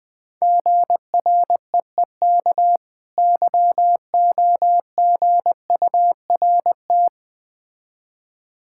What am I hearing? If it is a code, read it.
Morse code, heard as a keyed tone, about 20 wpm, 690 Hz. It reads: GREEK YOGURT
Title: Difficult Words with Hints - 20wpm